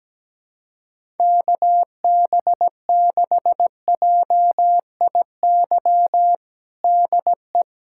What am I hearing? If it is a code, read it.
Morse code, heard as a keyed tone, about 17 wpm, 700 Hz. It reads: KB6JIY DE
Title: USA Callsign DE Callsign - 6 Characters - 17wpm